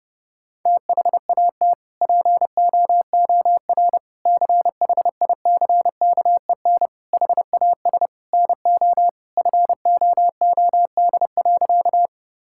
Morse code, heard as a keyed tone, about 30 wpm, 700 Hz. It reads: THAT POOR CHICKEN HAS NO FOOD.
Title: Sentences from Top 1000 Words - Review Part 5 - 30wpm